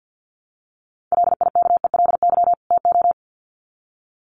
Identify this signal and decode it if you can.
Morse code, heard as a keyed tone, about 50 wpm, 700 Hz. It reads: LIKELY TO
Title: Sets of 2 Words 50wpm